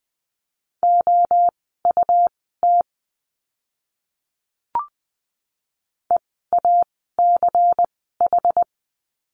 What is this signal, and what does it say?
Morse code, heard as a keyed tone, about 20 wpm, spaced as for 14 wpm, 700 Hz. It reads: OUT EACH
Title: 192 Lesson 31f - Letter v - Words 20wpm